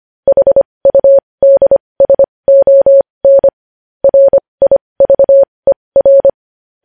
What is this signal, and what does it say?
Morse code, heard as a keyed tone, about 25 wpm, spaced as for 20 wpm, 550 Hz. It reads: HUDSON RIVER
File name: podcast / News headlines-1x-25x20